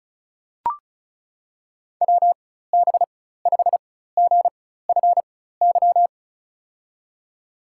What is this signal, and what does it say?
Morse code, heard as a keyed tone, about 36 wpm, spaced as for 16 wpm, 700 Hz. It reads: WB5GFY
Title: USA Call Signs - 6 Characters - 4x Character Spacing - 35wpm